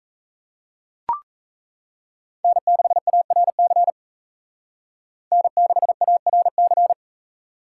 Morse code, heard as a keyed tone, about 36 wpm, 700 Hz. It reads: N6ARC N6ARC